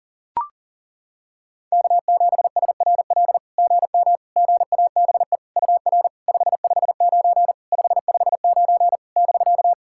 Morse code, heard as a keyed tone, about 40 wpm, 700 Hz. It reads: K7SRL GM GABE UR 559 559 <BK>
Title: SOTA QSO Practice 40wpm